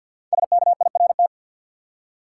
Morse code, heard as a keyed tone, about 53 wpm, 700 Hz. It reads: SKIRT
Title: Five Letter Words - 50wpm